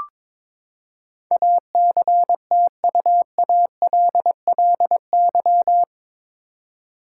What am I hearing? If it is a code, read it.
Morse code, heard as a keyed tone, about 21 wpm, 700 Hz. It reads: ACTUALLY
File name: practice / 22wpm / Eight Letter Words - 22wpm